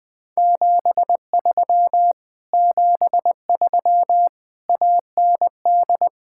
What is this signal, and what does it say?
Morse code, heard as a keyed tone, about 20 wpm, 700 Hz. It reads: 73 73 AND
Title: Speed-Racing Common QSO Phrases - 17wpm